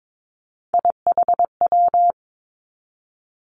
Morse code, heard as a keyed tone, about 22 wpm, 700 Hz. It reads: IHW
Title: Three Challenging Letter-Number Combinations 22wpm